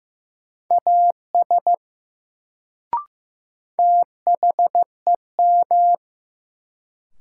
Morse code, heard as a keyed tone, about 15 wpm, 700 Hz. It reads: AS THEM
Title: Sentences from Top 100 Words - New Word Drill - 15wpm